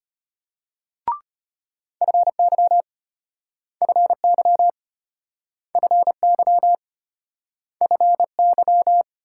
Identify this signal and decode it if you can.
Morse code, heard as a keyed tone, about 33 wpm, 700 Hz. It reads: FY FY FY FY